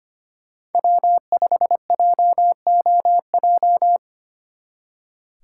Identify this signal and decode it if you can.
Morse code, heard as a keyed tone, about 31 wpm, 700 Hz. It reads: W5JOJ